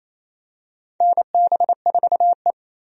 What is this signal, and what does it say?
Morse code, heard as a keyed tone, about 28 wpm, 700 Hz. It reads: NB4E
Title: USA Call Signs - 4 Characters - Repeated 2x - 28wpm